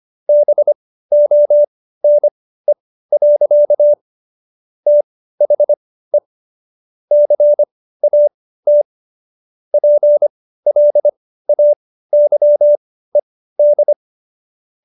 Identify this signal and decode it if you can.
Morse code, heard as a keyed tone, about 25 wpm, spaced as for 15 wpm, 590 Hz. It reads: BONE. THE CAT PLAYED